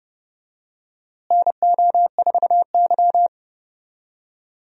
Morse code, heard as a keyed tone, about 30 wpm, 700 Hz. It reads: NO4Y
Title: USA Call Signs - 4 Characters - Repeated 2x - 30wpm